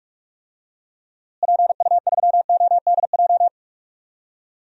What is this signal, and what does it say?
Morse code, heard as a keyed tone, about 46 wpm, 700 Hz. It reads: PU3ODJ
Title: DXCC-100 Call Signs - 6 Characters - Repeated 3x - 45wpm